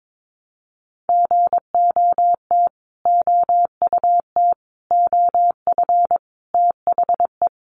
Morse code, heard as a keyed tone, about 22 wpm, 700 Hz. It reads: GOT OUT OF THE
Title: Sets of 5 Words Encore 22wpm